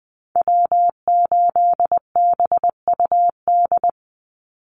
Morse code, heard as a keyed tone, about 20 wpm, 700 Hz. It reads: W8BUD
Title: Call Signs Repeated 2x - 20wpm